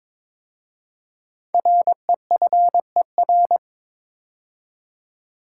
Morse code, heard as a keyed tone, about 22 wpm, 700 Hz. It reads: REFER
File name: 5 Letter Words - First 3 Letters Given - 22wpm